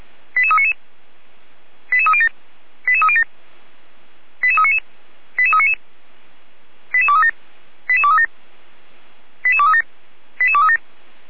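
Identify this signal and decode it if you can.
Five-tone selective call (ZVEI), heard as quick runs of five tones, tones of about 70 ms. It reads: 88399 88307 88307 88399 88399 88237 88237 88237 88237
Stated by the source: ZVEI five-tone selective call received on an SDR, recorded by feuerwehr-orlamuende.org (Feuerwehr Orlamünde).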